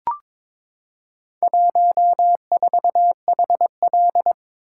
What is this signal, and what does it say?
Morse code, heard as a keyed tone, about 22 wpm, 700 Hz. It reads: 14HL
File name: practice / 22wpm / Four Letter-Number Combinations - 22wpm